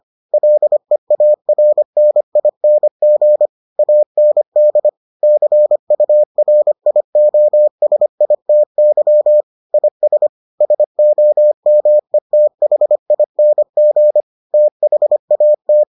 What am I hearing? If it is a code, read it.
Morse code, heard as a keyed tone, about 25 wpm, 600 Hz. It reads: LEARNING AND CURIOSITY IS SOMETHING THAT